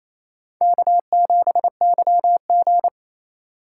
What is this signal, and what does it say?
Morse code, heard as a keyed tone, about 35 wpm, 700 Hz. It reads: K7YG